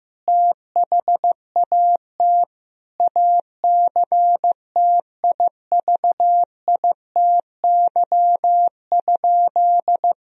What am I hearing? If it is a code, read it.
Morse code, heard as a keyed tone, about 15 wpm, 700 Hz. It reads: THAT ACTIVITY?
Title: Sentences from Top 1000 Words - Review Part 4 - 15wpm